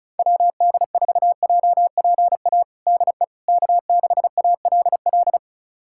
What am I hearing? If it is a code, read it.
Morse code, heard as a keyed tone, about 36 wpm, 700 Hz. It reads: WD4JPA DE K6ALL
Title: USA Callsign DE Callsign 35wpm